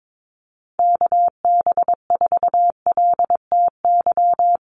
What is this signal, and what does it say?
Morse code, heard as a keyed tone, about 22 wpm, 700 Hz. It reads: KB4LTY
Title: USA Call Signs Repeated 2x 22wpm